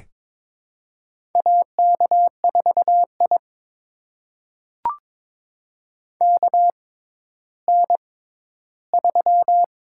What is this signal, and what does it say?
Morse code, heard as a keyed tone, about 21 wpm, spaced as for 12 wpm, 700 Hz. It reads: AK4I KN3
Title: USA Call Signs - 4 Characters - 6x Character Spacing - 22wpm